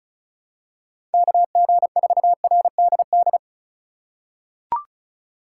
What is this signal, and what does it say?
Morse code, heard as a keyed tone, about 36 wpm, 700 Hz. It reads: KG4RDD
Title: USA Call Signs - 6 Characters - 4x Character Spacing - 35wpm